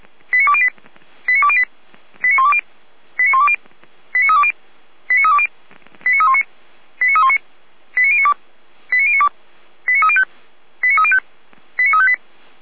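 Five-tone selective call (ZVEI), heard as quick runs of five tones, tones of about 70 ms. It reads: 89208 89208 89120 89120 89320 89320 89319 89319 89092 89092 89305 89305 89358
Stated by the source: ZVEI five-tone selective call, received on an SDR.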